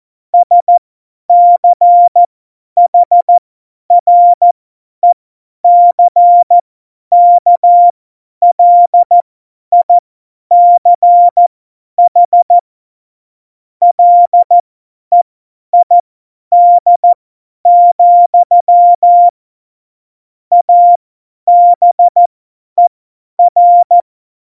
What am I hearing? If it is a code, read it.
Morse code, heard as a keyed tone, about 14 wpm, spaced as for 10 wpm, 700 Hz. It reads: SCHRECKLICH LEID, ABER
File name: Morsekurs / Witze / Witz 14. Mai 2019, 50 F70